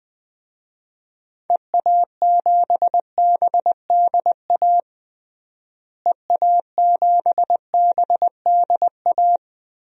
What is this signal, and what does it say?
Morse code, heard as a keyed tone, about 21 wpm, 700 Hz. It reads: EA7BDA EA7BDA
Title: DXCC-100 Call Signs - 6 Characters - Repeated 2x - 20wpm